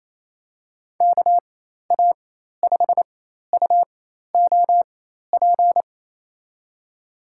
Morse code, heard as a keyed tone, about 28 wpm, spaced as for 13 wpm, 700 Hz. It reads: KA5UOP
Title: USA Call Signs - 6 Characters - 4x Character Spacing - 28wpm